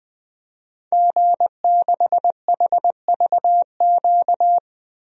Morse code, heard as a keyed tone, about 20 wpm, 700 Hz. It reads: G6HVQ
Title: DXCC-100 Call Signs - 5 Characters - 20wpm